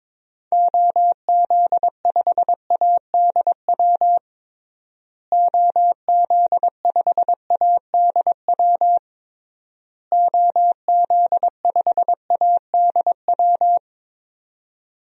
Morse code, heard as a keyed tone, about 22 wpm, 700 Hz. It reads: OZ5ADW OZ5ADW OZ5ADW
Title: DXCC-100 Call Signs - 6 Characters - Repeated 3x - 22wpm